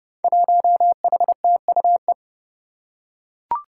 Morse code, heard as a keyed tone, about 30 wpm, 700 Hz. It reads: 1HTUE
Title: Five Challenging Letter-Number Combinations 30wpm